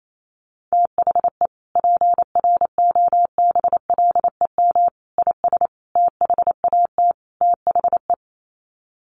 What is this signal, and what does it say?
Morse code, heard as a keyed tone, about 28 wpm, 700 Hz. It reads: THE PROBLEM IS THAT THE
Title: Sets of 5 Words Encore2 28wpm